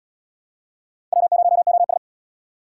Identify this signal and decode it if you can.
Morse code, heard as a keyed tone, about 70 wpm, 700 Hz. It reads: W0OS